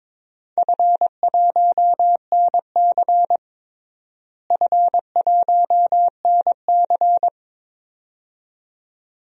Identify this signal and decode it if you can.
Morse code, heard as a keyed tone, about 22 wpm, 700 Hz. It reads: F1NC F1NC
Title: DXCC-100 Call Signs - 4 Characters - Repeated 2x - 22wpm